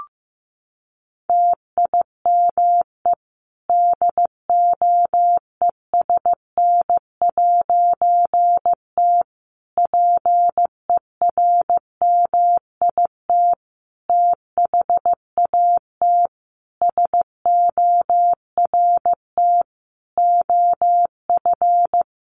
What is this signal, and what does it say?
Morse code, heard as a keyed tone, about 15 wpm, 700 Hz. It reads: TIME DOESN'T PERMIT THAT SORT OF
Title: A Sound Of Thunder 15wpm